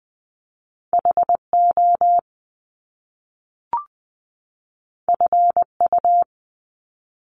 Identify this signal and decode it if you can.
Morse code, heard as a keyed tone, about 20 wpm, 700 Hz. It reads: HO FU
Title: Top Two Letter Combinations in Words 20wpm